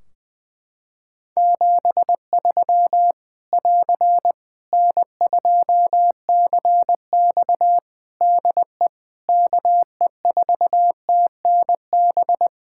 Morse code, heard as a keyed tone, about 20 wpm, 700 Hz. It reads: 73 <AR> N2CX DE KE4TNB
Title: Practice QSO - Exchange 4 - Ending - 20wpm